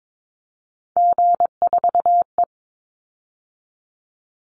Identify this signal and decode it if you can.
Morse code, heard as a keyed tone, about 22 wpm, 700 Hz. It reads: G4E
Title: Call Signs - 22wpm